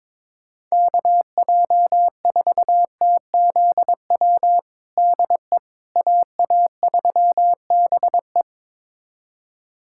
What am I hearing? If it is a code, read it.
Morse code, heard as a keyed tone, about 22 wpm, 700 Hz. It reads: KJ4TZW DE AA3BE
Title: USA Callsign DE Callsign 22wpm